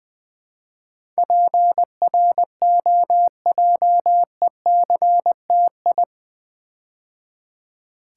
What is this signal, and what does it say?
Morse code, heard as a keyed tone, about 20 wpm, 700 Hz. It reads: PROJECTI
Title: Build A Word - 4 Letters At A Time - 10-Letter Words - 20wpm